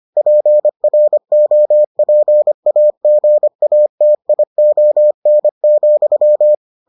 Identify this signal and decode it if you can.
Morse code, heard as a keyed tone, about 25 wpm, 600 Hz. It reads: PROPAGATION,